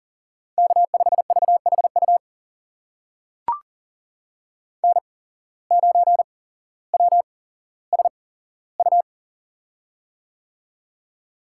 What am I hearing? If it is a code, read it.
Morse code, heard as a keyed tone, about 42 wpm, spaced as for 17 wpm, 700 Hz. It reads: K5VHU N9WSU
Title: USA Call Signs - 8x Character Spacing - 40wpm